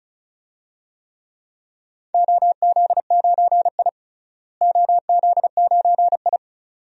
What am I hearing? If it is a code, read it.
Morse code, heard as a keyed tone, about 36 wpm, 700 Hz. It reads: OZ9I OZ9I
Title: DXCC-100 Call Signs - 4 Characters - Repeated 2x - 35wpm